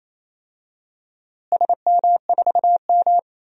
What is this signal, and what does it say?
Morse code, heard as a keyed tone, about 27 wpm, 700 Hz. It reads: SM4M
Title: DXCC-100 Call Signs - 4 Characters - Repeated 2x - 28wpm